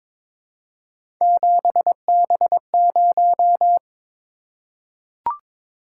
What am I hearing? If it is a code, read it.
Morse code, heard as a keyed tone, about 22 wpm, 700 Hz. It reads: 7B0
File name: practice / 22wpm / Three Letter-Number Combinations - 22wpm